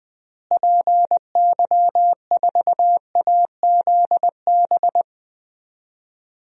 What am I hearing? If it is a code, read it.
Morse code, heard as a keyed tone, about 20 wpm, 700 Hz. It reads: PY4AZB
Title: DXCC-100 Call Signs - 6 Characters - Repeated 3x - 20wpm